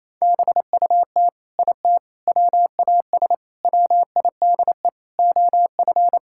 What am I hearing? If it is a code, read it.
Morse code, heard as a keyed tone, about 28 wpm, 700 Hz. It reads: BUT IT WAS WIDE OF
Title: Sentences from Top 1000 Words - Review Part 6 - 28wpm